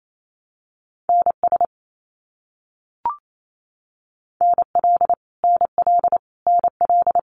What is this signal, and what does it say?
Morse code, heard as a keyed tone, about 28 wpm, 700 Hz. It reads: NS NL NL NL
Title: Canadian Provinces and Territories - Repeated 3x - 28wpm